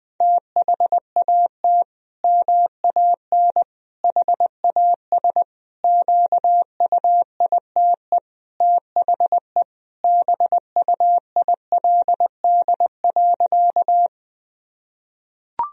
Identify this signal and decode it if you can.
Morse code, heard as a keyed tone, about 20 wpm, 700 Hz. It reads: THAT MAN HAS QUITE THE BUILD.
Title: Sentences from Top 1000 Words - Review Part 1 - 20wpm